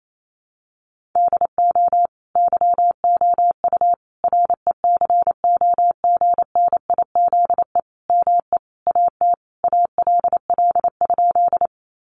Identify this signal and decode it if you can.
Morse code, heard as a keyed tone, about 28 wpm, 700 Hz. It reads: DO YOU RECOGNIZE ME AT ALL?
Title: Sentences from Top 1400 Words 28wpm